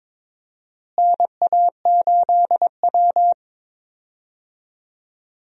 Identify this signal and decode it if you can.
Morse code, heard as a keyed tone, about 22 wpm, 700 Hz. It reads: NA8W
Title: USA Call Signs - 4 Characters - Repeated 3x - 22wpm